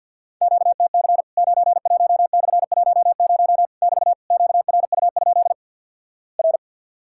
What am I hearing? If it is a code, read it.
Morse code, heard as a keyed tone, about 50 wpm, 700 Hz. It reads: QTC 91/10 <BT> QRU? R
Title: Worked All Europe DX Contest - QTC Exchange - 50wpm